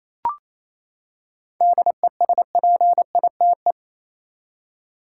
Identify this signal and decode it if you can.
Morse code, heard as a keyed tone, about 28 wpm, 700 Hz. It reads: DESPITE
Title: Seven Letter Words - 28wpm